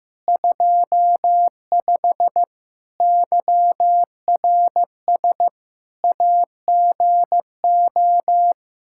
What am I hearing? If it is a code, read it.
Morse code, heard as a keyed tone, about 15 wpm, 700 Hz. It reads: 25 YRS AGO
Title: K1USN SST QSO Practice 15wpm